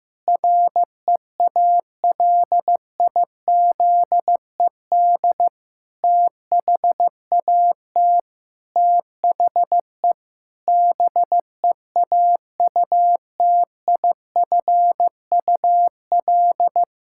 Morse code, heard as a keyed tone, about 16 wpm, 700 Hz. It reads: REALIZED THAT THE BEAUTIFUL
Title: Pangram Challenge v2 - 15wpm